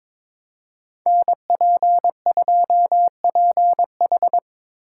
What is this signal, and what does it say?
Morse code, heard as a keyed tone, about 26 wpm, 700 Hz. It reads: NP2PH
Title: Speed Racing - USA 2x2 Call Signs 22wpm